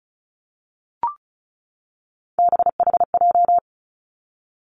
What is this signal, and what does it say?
Morse code, heard as a keyed tone, about 34 wpm, 700 Hz. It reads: BHJ
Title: Three Challenging Letter-Number Combinations 35wpm